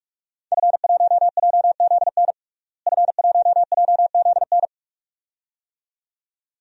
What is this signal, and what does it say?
Morse code, heard as a keyed tone, about 46 wpm, 700 Hz. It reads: F1JZN F1JZN
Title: DXCC-100 Call Signs - 5 Characters - Repeated 2x - 45wpm